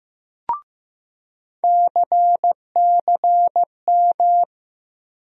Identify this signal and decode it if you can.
Morse code, heard as a keyed tone, about 15 wpm, 700 Hz. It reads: CCM
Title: USA Call Sign Postfixes - 3 Letters 15wpm